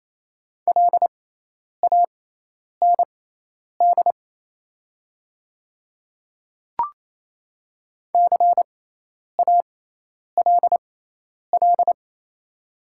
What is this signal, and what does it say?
Morse code, heard as a keyed tone, about 28 wpm, spaced as for 10 wpm, 700 Hz. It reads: LAND CALL
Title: 6x Character Spacing - Top 500 Words - 28wpm